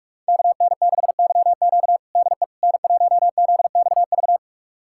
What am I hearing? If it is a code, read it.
Morse code, heard as a keyed tone, about 46 wpm, 700 Hz. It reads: KN6YQ DE N1ZXV
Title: USA Callsign DE Callsign - 5 Characters - 45wpm